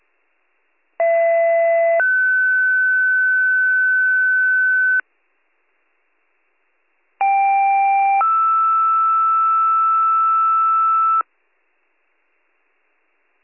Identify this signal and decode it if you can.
Two-tone page, heard as a one-second tone, then a three-second tone, about 670 then 1500 Hz, about 790 then 1320 Hz. It is heard from 1.0 to 11.2 s.